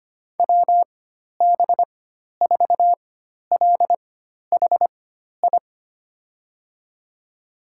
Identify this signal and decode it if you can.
Morse code, heard as a keyed tone, about 25 wpm, spaced as for 12 wpm, 700 Hz. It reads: WB4LHI
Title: USA Call Signs - 6 Characters - 4x Character Spacing - 25wpm